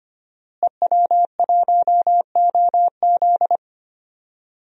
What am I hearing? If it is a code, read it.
Morse code, heard as a keyed tone, about 25 wpm, 700 Hz. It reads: EW1OZ
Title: DXCC-100 Call Signs - 5 Characters - Repeated 3x - 25wpm